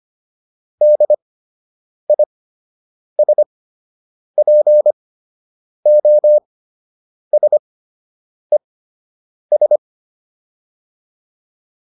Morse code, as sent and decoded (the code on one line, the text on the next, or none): -.. .. ... .--. --- ... ...
DISPOSES